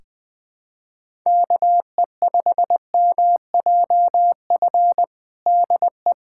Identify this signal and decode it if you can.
Morse code, heard as a keyed tone, about 20 wpm, 700 Hz. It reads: KE5MJF DE
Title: USA Callsign DE Callsign - 6 Characters - 20wpm